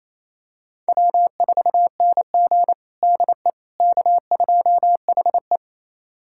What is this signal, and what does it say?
Morse code, heard as a keyed tone, about 27 wpm, 690 Hz. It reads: W4NG DE K2HE
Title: USA Callsign DE Callsign - 4 Characters - 28wpm